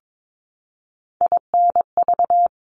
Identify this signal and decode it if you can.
Morse code, heard as a keyed tone, about 22 wpm, 700 Hz. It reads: INV
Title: Top 1001-1250 Words 22wpm